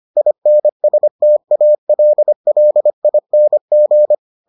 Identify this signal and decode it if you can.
Morse code, heard as a keyed tone, about 25 wpm, 600 Hz. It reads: INSTALLING